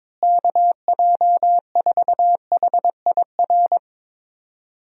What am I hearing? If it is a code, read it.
Morse code, heard as a keyed tone, about 22 wpm, 700 Hz. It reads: KJ4HIR
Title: USA Call Signs - 6 Characters - Repeated 3x - 22wpm